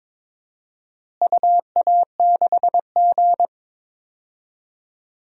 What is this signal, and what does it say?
Morse code, heard as a keyed tone, about 23 wpm, 700 Hz. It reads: UA6G
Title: DXCC-100 Call Signs - 4 Characters - Repeated 3x - 22wpm